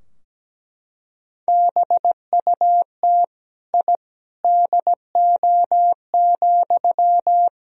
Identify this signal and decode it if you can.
Morse code, heard as a keyed tone, about 17 wpm, 700 Hz. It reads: BUT I DO,
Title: Sentences from Top 1000 Words - Review Part 2 - 17wpm